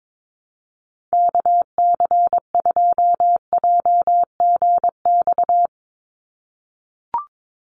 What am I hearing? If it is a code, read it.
Morse code, heard as a keyed tone, about 22 wpm, 700 Hz. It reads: KC2JGX
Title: USA Call Signs 22wpm